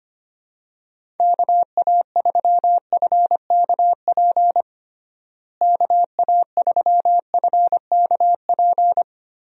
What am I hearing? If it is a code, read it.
Morse code, heard as a keyed tone, about 25 wpm, 700 Hz. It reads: KA3FKP KA3FKP
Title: USA Call Signs - 6 Characters - Repeated 3x - 25wpm